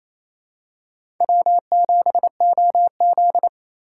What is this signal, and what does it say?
Morse code, heard as a keyed tone, about 28 wpm, 700 Hz. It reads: W7OZ
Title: USA Call Signs - 4 Characters - 3x Character Spacing - 28wpm